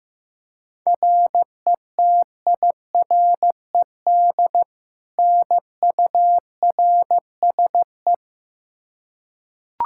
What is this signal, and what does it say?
Morse code, heard as a keyed tone, about 15 wpm, 700 Hz. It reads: RETIRED NURSE
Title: Practice QSO - Exchange 3 - More Info - 15wpm